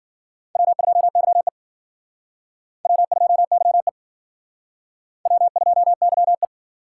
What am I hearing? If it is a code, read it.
Morse code, heard as a keyed tone, about 50 wpm, 700 Hz. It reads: W2YE W2YE W2YE